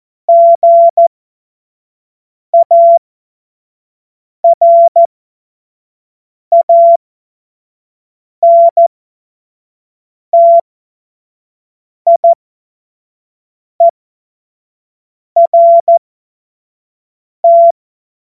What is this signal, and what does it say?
Morse code, heard as a keyed tone, about 14 wpm, spaced as for 5 wpm, 680 Hz. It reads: GARANTIERT
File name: Morsekurs / Witze / Witz 03. August 2021, 25 F70